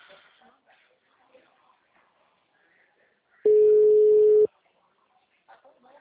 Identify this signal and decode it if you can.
Ringback tone (Europe, 425 Hz), with speech.